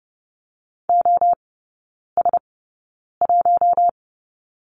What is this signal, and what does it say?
Morse code, heard as a keyed tone, about 30 wpm, spaced as for 9 wpm, 700 Hz. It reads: OS1
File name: Lesson 18e - Letter m - Three Characters 30wpm